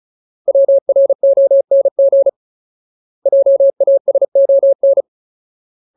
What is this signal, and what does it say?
Morse code, heard as a keyed tone, about 35 wpm, 550 Hz. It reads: WRONG JASON
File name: Ultimate Word List / 35WPM Ultimate